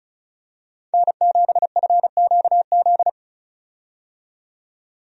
Morse code, heard as a keyed tone, about 36 wpm, 700 Hz. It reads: N7FQZ
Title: USA Call Signs - 5 Characters - 35wpm